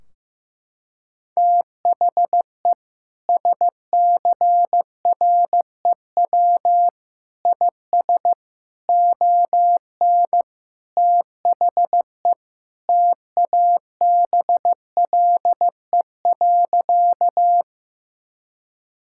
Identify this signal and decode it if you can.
Morse code, heard as a keyed tone, about 15 wpm, 700 Hz. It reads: THE SCREW IS ON THE TABLE.
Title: Sentences from Top 1600 Words 15wpm